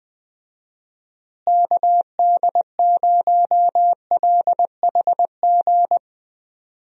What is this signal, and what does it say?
Morse code, heard as a keyed tone, about 20 wpm, 700 Hz. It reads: KD0LHG
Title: USA Call Signs - 6 Characters - Repeated 2x - 20wpm